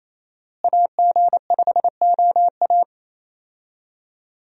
Morse code, heard as a keyed tone, about 28 wpm, 700 Hz. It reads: AG5OA
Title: USA Call Signs - 5 Characters - Repeated 2x - 28wpm